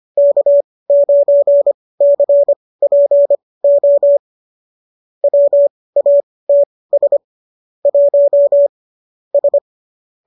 Morse code, heard as a keyed tone, about 25 wpm, spaced as for 18 wpm, 590 Hz. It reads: K9CPO WATS 1 S